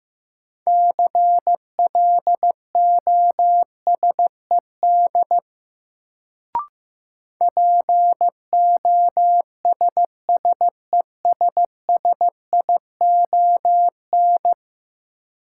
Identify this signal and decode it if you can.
Morse code, heard as a keyed tone, about 15 wpm, 700 Hz. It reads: CLOSED POSSESSION
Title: Sentences from Top 1400 Words - New Word Drill - 15wpm